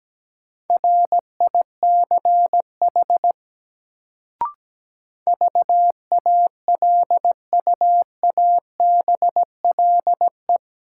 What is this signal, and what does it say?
Morse code, heard as a keyed tone, about 17 wpm, 700 Hz. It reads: RICH VALUABLE
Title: Sentences from Top 1100 Words - New Word Drill - 17wpm